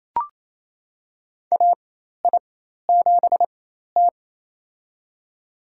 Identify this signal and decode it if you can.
Morse code, heard as a keyed tone, about 28 wpm, spaced as for 13 wpm, 700 Hz. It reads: AI7T
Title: USA Call Signs - 4 Characters - 4x Character Spacing - 28wpm